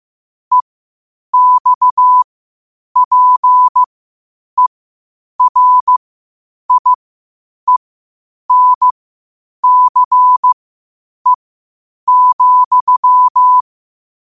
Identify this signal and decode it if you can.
Morse code, heard as a keyed tone, about 15 wpm, spaced as for 8 wpm, 1000 Hz. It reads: EXPERIENCE,
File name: qotdfiles / 8wpm